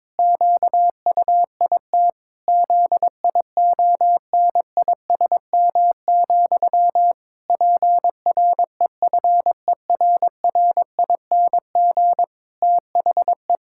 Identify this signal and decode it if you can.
Morse code, heard as a keyed tone, about 22 wpm, 700 Hz. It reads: QUIT ZIONISM, PREFERRING THE
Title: Pangram Challenge v2 - 22wpm